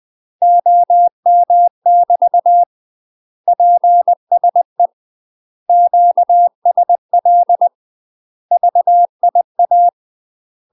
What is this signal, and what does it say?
Morse code, heard as a keyed tone, about 20 wpm, 700 Hz. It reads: OM <BT> PSE QSL VIA
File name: UK MASTER QSOs TRAINING/20WPM